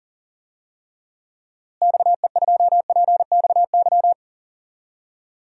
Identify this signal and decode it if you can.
Morse code, heard as a keyed tone, about 40 wpm, 700 Hz. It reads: XE2PXY
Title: DXCC-100 Call Signs - 6 Characters - Repeated 3x - 40wpm